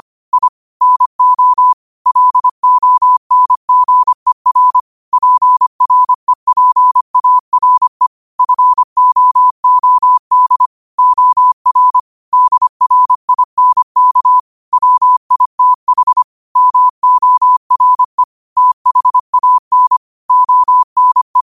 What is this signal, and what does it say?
Morse code, heard as a keyed tone, about 25 wpm, 1000 Hz. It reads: I NO LONGER PREPARE FOOD OR DRINK WITH MORE THAN ONE
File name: qotdfiles / 25wpm